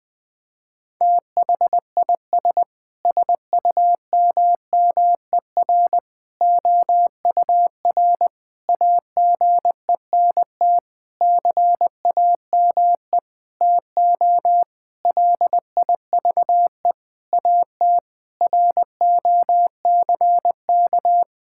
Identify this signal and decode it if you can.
Morse code, heard as a keyed tone, about 20 wpm, 700 Hz. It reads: THIS SUMMER OUR AGENT CAME TO LIVE AT ROCK
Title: Sentences from Top 1600 Words 20wpm